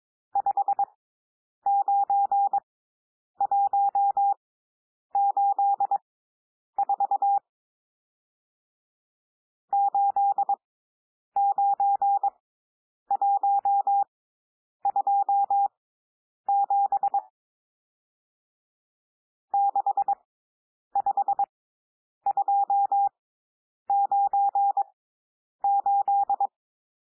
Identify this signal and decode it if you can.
Morse code, heard as a keyed tone, about 22 wpm, spaced as for 9 wpm, 800 Hz. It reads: 59184 89127 65298
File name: cwmp3 / N1 SOUND NUMBERS 22 wpm